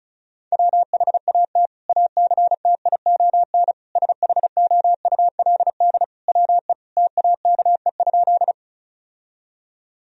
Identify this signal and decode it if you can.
Morse code, heard as a keyed tone, about 36 wpm, 700 Hz. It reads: WHAT ACTION SHOULD WE TAKE?
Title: Sentences from Top 1000 Words - Review Part 4 - 35wpm